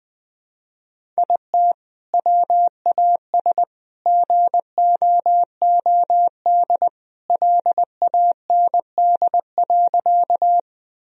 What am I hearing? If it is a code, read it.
Morse code, heard as a keyed tone, about 20 wpm, 700 Hz. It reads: IT WAS GOOD LAND.
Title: Sentences from Top 1000 Words - Review Part 1 - 20wpm